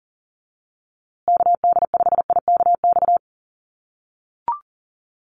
Morse code, heard as a keyed tone, about 40 wpm, 700 Hz. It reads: KD5IKX
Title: USA Call Signs Repeated 2x 40wpm